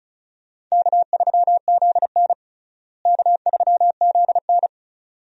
Morse code, heard as a keyed tone, about 36 wpm, 700 Hz. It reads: K3ZN K3ZN
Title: USA Call Signs - 4 Characters - Repeated 2x - 35wpm